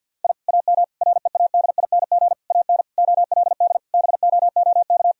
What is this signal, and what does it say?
Morse code, heard as a keyed tone, about 50 wpm, 700 Hz. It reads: I AM READING AN OLD BOOK
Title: Sentences from Top 1000 Words - Review Part 4 - 50wpm